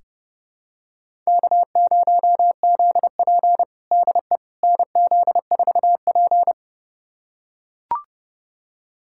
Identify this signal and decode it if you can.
Morse code, heard as a keyed tone, about 30 wpm, 700 Hz. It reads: K0ZP DE NZ4P
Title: USA Callsign DE Callsign - 4 Characters - 30wpm